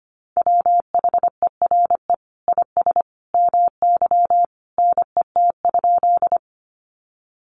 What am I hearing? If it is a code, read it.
Morse code, heard as a keyed tone, about 25 wpm, 700 Hz. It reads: WHERE IS MY NET?
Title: Sentences from Top 1400 Words 25wpm